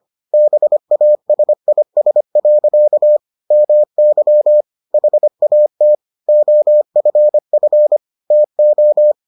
Morse code, heard as a keyed tone, about 25 wpm, 600 Hz. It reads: BASIS. MY HAT OFF TO